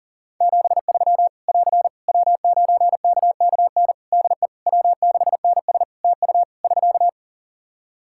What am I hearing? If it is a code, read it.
Morse code, heard as a keyed tone, about 40 wpm, 700 Hz. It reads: 73 <AR> W9KKN DE W6NS TU <SK>
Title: Practice QSO - Exchange 4 - Ending - 40wpm